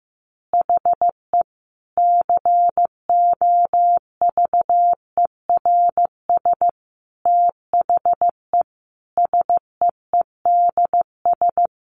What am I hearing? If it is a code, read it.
Morse code, heard as a keyed tone, about 15 wpm, 700 Hz. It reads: HE COVERS THE SEEDS
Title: Easy Phrases 15wpm